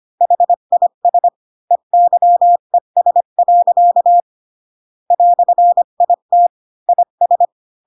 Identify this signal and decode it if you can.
Morse code, heard as a keyed tone, about 25 wpm, 700 Hz. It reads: HIS EYES. "IT IS